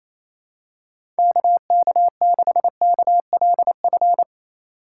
Morse code, heard as a keyed tone, about 28 wpm, 700 Hz. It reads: KK6KLF
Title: USA Call Signs - 6 Characters - 2x Character Spacing - 28wpm